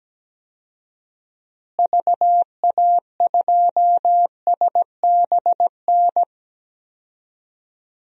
Morse code, heard as a keyed tone, about 17 wpm, 700 Hz. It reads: VA2SBN
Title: DXCC-100 Call Signs - 6 Characters - Repeated 2x - 17wpm